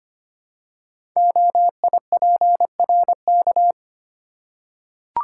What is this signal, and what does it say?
Morse code, heard as a keyed tone, about 25 wpm, 700 Hz. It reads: OIPRK
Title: Code Groups - 5 Letters - 25wpm